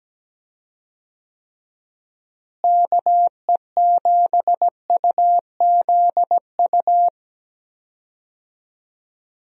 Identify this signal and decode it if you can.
Morse code, heard as a keyed tone, about 17 wpm, 700 Hz. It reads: KE7UZU
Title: USA Call Signs - 6 Characters - Repeated 3x - 17wpm